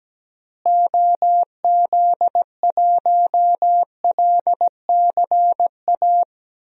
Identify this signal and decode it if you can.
Morse code, heard as a keyed tone, about 17 wpm, 700 Hz. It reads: OZ1LCA
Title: DXCC-100 Call Signs - 6 Characters - Repeated 2x - 17wpm